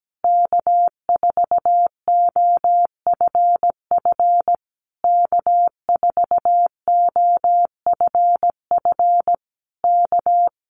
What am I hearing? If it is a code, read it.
Morse code, heard as a keyed tone, about 17 wpm, 700 Hz. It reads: K4OFF K4OFF K
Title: Common QSO Phrases Encore 17wpm